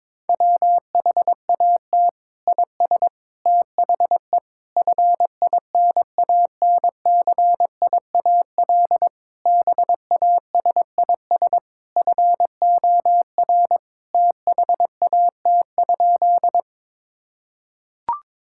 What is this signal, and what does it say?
Morse code, heard as a keyed tone, about 22 wpm, 700 Hz. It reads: WHAT IS THE FINANCIAL BASIS FOR THAT?
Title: Sentences from Top 1000 Words - Review Part 5 - 22wpm